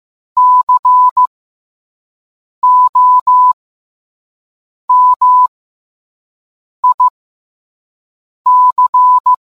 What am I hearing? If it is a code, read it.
Morse code, heard as a keyed tone, about 15 wpm, spaced as for 5 wpm, 1000 Hz. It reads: COMIC